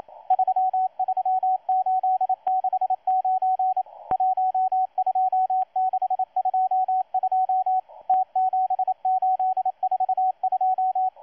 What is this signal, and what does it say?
Morse code, heard as a keyed tone, about 27 wpm, 750 Hz, with radio noise. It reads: 33869 12622 T7842
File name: M12-2016-07-07-1310utc-13926khz-error-restart